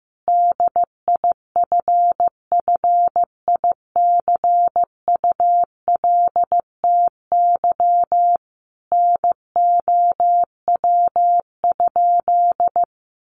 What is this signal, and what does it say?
Morse code, heard as a keyed tone, about 15 wpm, 700 Hz. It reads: DIFFICULTY NOW?
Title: Sentences from Top 1500 Words 15wpm